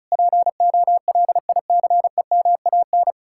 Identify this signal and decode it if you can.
Morse code, heard as a keyed tone, about 48 wpm, spaced as for 42 wpm, 700 Hz. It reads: POLICEMAN
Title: Speed-Racing Common QSO Phrases Encore - 35wpm